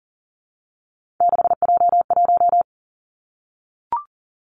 Morse code, heard as a keyed tone, about 40 wpm, 700 Hz. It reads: 6J1
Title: Three Challenging Letter-Number Combinations 40wpm